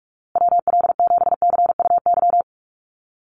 Morse code, heard as a keyed tone, about 44 wpm, 700 Hz. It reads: WL7CUY
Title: USA Call Signs Repeated 2x 45wpm